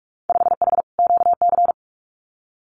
Morse code, heard as a keyed tone, about 46 wpm, 690 Hz. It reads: 5H QC
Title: Winter Field Day Exchanges Encore 45wpm